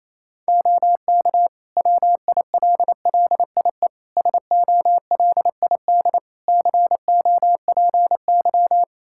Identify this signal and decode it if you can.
Morse code, heard as a keyed tone, about 28 wpm, 700 Hz. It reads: OK WILLIE SOLID COPY
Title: Practice QSO - Exchange 3 - More Info - 28wpm